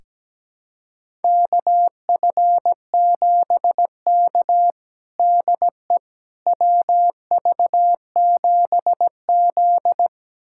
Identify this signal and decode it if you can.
Morse code, heard as a keyed tone, about 17 wpm, 700 Hz. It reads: KF7K DE WV7Z
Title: USA Callsign DE Callsign - 4 Characters - 17wpm